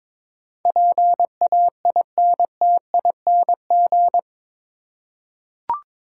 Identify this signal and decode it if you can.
Morse code, heard as a keyed tone, about 22 wpm, 700 Hz. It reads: PAINTING
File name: practice / 22wpm / Eight Letter Words - 22wpm